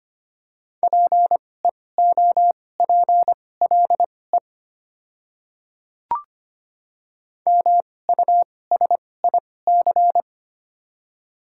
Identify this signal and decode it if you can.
Morse code, heard as a keyed tone, about 25 wpm, spaced as for 18 wpm, 700 Hz. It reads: PEOPLE MUSIC
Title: 2x Character Spacing - Top 300 Words - 25wpm